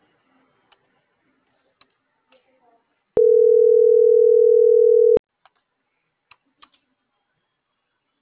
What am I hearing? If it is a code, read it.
Ringback tone (North America), 440 + 480 Hz.